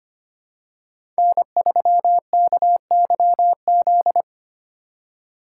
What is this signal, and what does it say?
Morse code, heard as a keyed tone, about 25 wpm, 700 Hz. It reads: N3KYZ
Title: USA Call Signs - 5 Characters - 4x Character Spacing - 25wpm